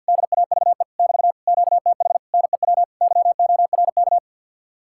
Morse code, heard as a keyed tone, about 59 wpm, 700 Hz. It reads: DAVE <BT> QTH NEW YORK
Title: Speed-Racing Common QSO Phrases - 50wpm